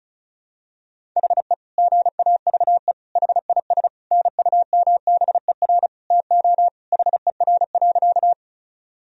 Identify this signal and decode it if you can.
Morse code, heard as a keyed tone, about 36 wpm, 700 Hz. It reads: HE GAVE HIS NUMBER TO HER.
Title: Sentences from Top 1000 Words - Review Part 1 - 35wpm